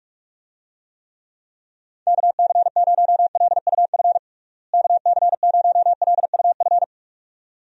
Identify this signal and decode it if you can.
Morse code, heard as a keyed tone, about 46 wpm, 700 Hz. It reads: KC0LUF KC0LUF
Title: USA Call Signs - 6 Characters - Repeated 2x - 45wpm